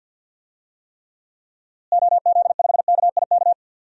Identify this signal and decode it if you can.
Morse code, heard as a keyed tone, about 52 wpm, 700 Hz. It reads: OZ5KIK
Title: DXCC-100 Call Signs - 6 Characters - Repeated 2x - 50wpm